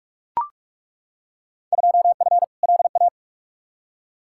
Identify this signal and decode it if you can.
Morse code, heard as a keyed tone, about 42 wpm, 700 Hz. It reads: 2F LA
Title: Field Day Exchanges - 45wpm